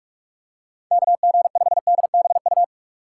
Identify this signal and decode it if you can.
Morse code, heard as a keyed tone, about 44 wpm, 700 Hz. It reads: KG5DBU
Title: USA Call Signs - 4x Character Spacing - 45wpm